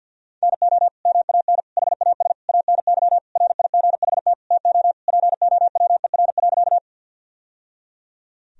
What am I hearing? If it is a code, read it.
Morse code, heard as a keyed tone, about 50 wpm, 700 Hz. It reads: NO MAN HAS ANY RIGHT TO POWER.
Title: Sentences from Top 1000 Words - Review Part 2 - 50wpm